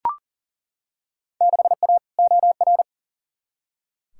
Morse code, heard as a keyed tone, about 37 wpm, 700 Hz. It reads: 6A OR
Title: Field Day Exchanges - 40wpm